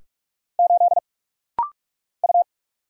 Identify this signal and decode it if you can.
Morse code, heard as a keyed tone, about 46 wpm, 700 Hz. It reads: 8 U
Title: Single Letter-Number Alt - Rapid-Fire 45wpm